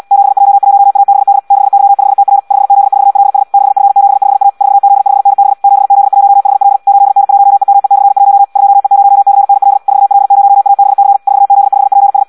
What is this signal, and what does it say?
Morse code, heard as a keyed tone, about 74 wpm, 800 Hz. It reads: 7J1ABD 6Y5TR 5Z4LS 6F75A 4Z5NB 7Z1HL 8N1EME/1 2E0CNL 4L0ABC 4Z5PS
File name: Audio practica CW